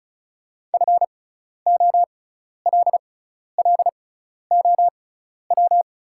Morse code, heard as a keyed tone, about 36 wpm, spaced as for 12 wpm, 700 Hz. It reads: FOLLOW
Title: 6x Character Spacing - Top 500 Words - 35wpm